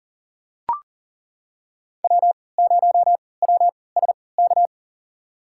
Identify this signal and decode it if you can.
Morse code, heard as a keyed tone, about 40 wpm, spaced as for 23 wpm, 700 Hz. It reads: W0WSK
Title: USA Call Signs - 5 Characters - 3x Character Spacing - 40wpm